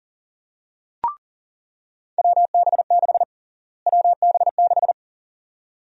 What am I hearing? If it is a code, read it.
Morse code, heard as a keyed tone, about 40 wpm, 700 Hz. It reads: WB6 WB6
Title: USA Call Sign Prefixes - Repeated 2x - 40wpm